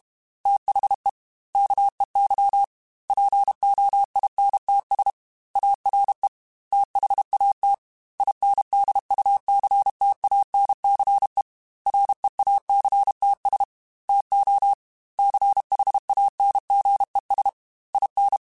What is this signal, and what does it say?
Morse code, heard as a keyed tone, about 32 wpm, 800 Hz. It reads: THE KEY POINTS ARE THAT INDUCTANCE REACTS TO CHANGES IN
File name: Technical Subjects / Technical Subjects-mp3s / Reactance @30